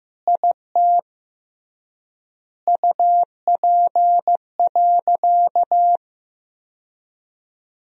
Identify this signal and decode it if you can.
Morse code, heard as a keyed tone, about 16 wpm, 700 Hz. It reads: IT UP.
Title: 3x Word-Spacing - Sentences from Top 100 Words - 15wpm